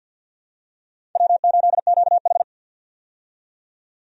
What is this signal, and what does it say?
Morse code, heard as a keyed tone, about 50 wpm, 700 Hz. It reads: W8QH
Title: USA Call Signs - 4 Characters - 50wpm